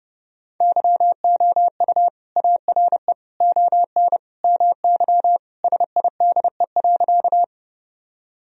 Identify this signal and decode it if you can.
Morse code, heard as a keyed tone, about 30 wpm, 700 Hz. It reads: YOU ARE ON MY SIDE.
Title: Sentences from Top 1000 Words - Review Part 1 - 30wpm